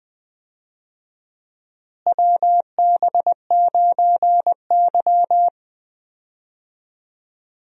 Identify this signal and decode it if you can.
Morse code, heard as a keyed tone, about 20 wpm, 700 Hz. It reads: WB9Y
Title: USA Call Signs - 4 Characters - Repeated 3x - 20wpm